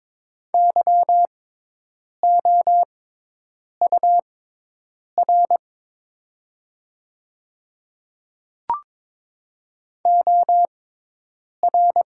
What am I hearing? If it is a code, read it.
Morse code, heard as a keyed tone, about 22 wpm, spaced as for 8 wpm, 700 Hz. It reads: YOUR OR
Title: 6x Character Spacing - Top 100 Words - 22wpm